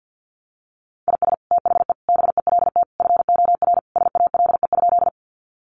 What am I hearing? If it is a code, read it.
Morse code, heard as a keyed tone, about 50 wpm, 700 Hz. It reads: IS THE BELT FOR SALE?
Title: Sentences from Top 1400 Words 50wpm